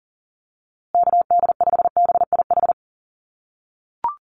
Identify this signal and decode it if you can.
Morse code, heard as a keyed tone, about 37 wpm, 700 Hz. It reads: KD5BIH
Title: USA Call Signs Repeated 3x 40wpm